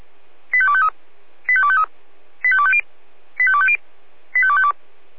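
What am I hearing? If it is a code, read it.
Five-tone selective call (ZVEI), heard as quick runs of five tones, tones of about 70 ms. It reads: 85253 85253 85250 85250 85252
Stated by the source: ZVEI five-tone selective call received with an SDR, fire brigade feuerwehr-orlamuende.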